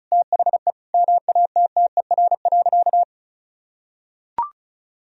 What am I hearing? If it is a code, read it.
Morse code, heard as a keyed tone, about 36 wpm, 700 Hz. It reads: THE MATTER.
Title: Sentences from Top 1000 Words - Review Part 4 - 35wpm